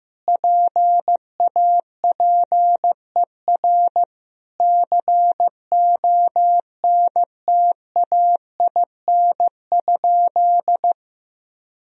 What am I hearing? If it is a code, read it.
Morse code, heard as a keyed tone, about 16 wpm, 700 Hz. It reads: PAPER CONTAIN?
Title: Sentences from Top 1000 Words - Review Part 2 - 15wpm